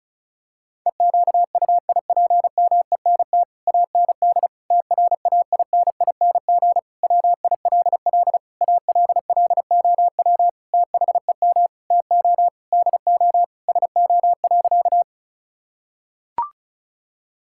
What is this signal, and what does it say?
Morse code, heard as a keyed tone, about 34 wpm, 700 Hz. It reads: EQUIPMENT AND TRAINING WILL ALLOW THEM TO DO SO.
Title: Sentences from Top 1000 Words - Review Part 5 - 35wpm